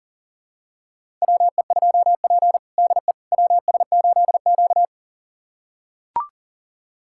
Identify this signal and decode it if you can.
Morse code, heard as a keyed tone, about 40 wpm, 700 Hz. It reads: WE2P DE WS8Q
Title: USA Callsign DE Callsign - 4 Characters - 40wpm